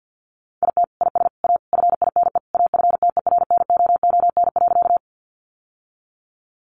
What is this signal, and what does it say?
Morse code, heard as a keyed tone, about 50 wpm, 700 Hz. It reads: IT IS A FINE AFTERNOON.
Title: Sentences from Top 1500 Words 50wpm